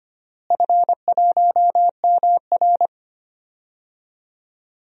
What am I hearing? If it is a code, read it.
Morse code, heard as a keyed tone, about 25 wpm, 700 Hz. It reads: F1MR
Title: DXCC-100 Call Signs - 4 Characters - Repeated 2x - 25wpm